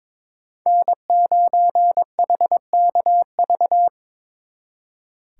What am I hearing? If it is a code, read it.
Morse code, heard as a keyed tone, about 25 wpm, 700 Hz. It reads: N9HKV
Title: Speed Racing - USA 1x3 Call Signs 22wpm